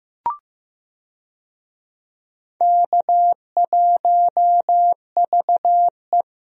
Morse code, heard as a keyed tone, about 16 wpm, 700 Hz. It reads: K1VE
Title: USA Call Signs - 4 Characters - 15wpm